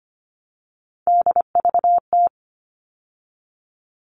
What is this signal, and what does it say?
Morse code, heard as a keyed tone, about 25 wpm, 700 Hz. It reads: DVT
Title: All Cut Numbers - 3-Digit 25wpm